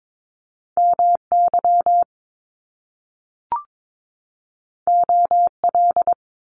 Morse code, heard as a keyed tone, about 22 wpm, 700 Hz. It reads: MY OL
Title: Top Two Letter Combinations in Words 22wpm